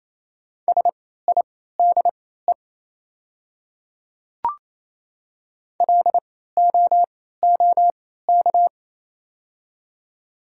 Morse code, heard as a keyed tone, about 28 wpm, spaced as for 16 wpm, 700 Hz. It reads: SIDE LOOK
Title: 3x Character Spacing - Top 100 Words - 28wpm